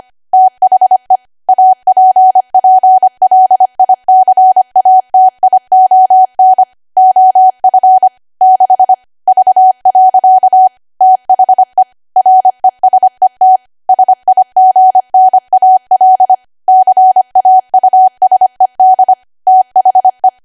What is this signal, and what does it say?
Morse code, heard as a keyed tone, about 25 wpm, 750 Hz. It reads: THE APPLICATION OF 6 V. THE RESET SIGNAL CAUSED THE